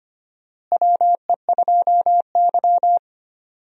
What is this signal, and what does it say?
Morse code, heard as a keyed tone, about 25 wpm, 700 Hz. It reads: WE2Y
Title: USA Call Signs - 4 Characters - Repeated 2x - 25wpm